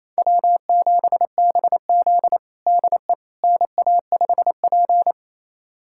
Morse code, heard as a keyed tone, about 28 wpm, 700 Hz. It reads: W7BZ DE NA5P
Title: USA Callsign DE Callsign - 4 Characters - 28wpm